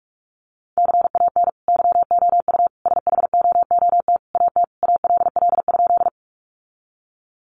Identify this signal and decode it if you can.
Morse code, heard as a keyed tone, about 44 wpm, 700 Hz. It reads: CAN YOU SHOOT AT ALL?
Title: Sentences from Top 1200 Words 45wpm